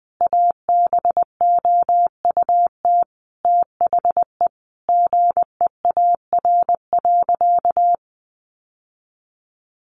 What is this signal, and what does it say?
Morse code, heard as a keyed tone, about 20 wpm, 700 Hz. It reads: ABOUT THE GEAR.
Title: Sentences from Top 1400 Words 20wpm